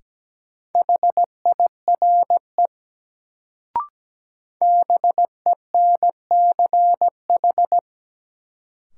Morse code, heard as a keyed tone, about 17 wpm, 700 Hz. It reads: HIRE BENCH
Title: Sentences from Top 1400 Words - New Word Drill - 17wpm